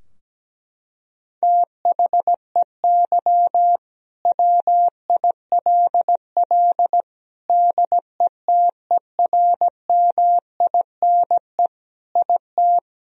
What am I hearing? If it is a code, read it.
Morse code, heard as a keyed tone, about 17 wpm, 700 Hz. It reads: THEY WILL DETERMINE IT
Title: Sentences from Top 1000 Words - Review Part 4 - 17wpm